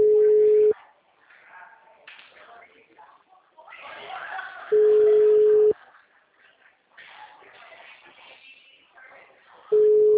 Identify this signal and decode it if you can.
Ringback tone (Europe, 425 Hz), with speech.
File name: out- -2013- - - .271